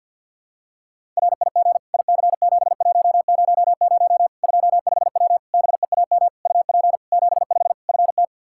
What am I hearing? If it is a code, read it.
Morse code, heard as a keyed tone, about 52 wpm, 700 Hz. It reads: RIG IC7100 25W BEAM UP 75 FT
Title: Practice QSO - Exchange 2 - Info - 50wpm